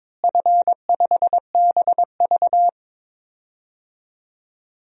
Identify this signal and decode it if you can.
Morse code, heard as a keyed tone, about 22 wpm, 700 Hz. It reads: F5BV
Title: DXCC-100 Call Signs - 4 Characters - Repeated 2x - 22wpm